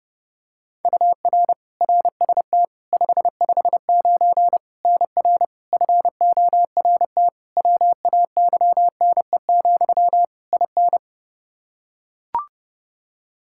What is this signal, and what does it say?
Morse code, heard as a keyed tone, about 30 wpm, 700 Hz. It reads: UR RST 559 NR FORT WAYNE, IN
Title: Practice QSO - Exchange 1 - Beginning - 30wpm